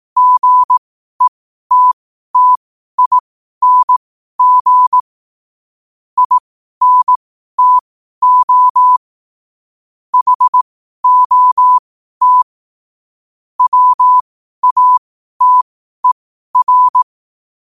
Morse code, heard as a keyed tone, about 18 wpm, spaced as for 12 wpm, 1000 Hz. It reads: GETTING INTO HOT WATER
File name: qotdfiles / 12wpm